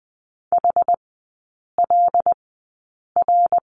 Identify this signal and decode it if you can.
Morse code, heard as a keyed tone, about 20 wpm, spaced as for 8 wpm, 700 Hz. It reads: HLR
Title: 127 Lesson 21e - Number 6 - Three Characters 20wpm